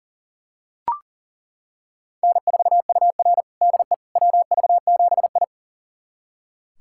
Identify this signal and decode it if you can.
Morse code, heard as a keyed tone, about 40 wpm, 700 Hz. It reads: N4UR DE WV7I
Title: USA Callsign DE Callsign - 4 Characters - 40wpm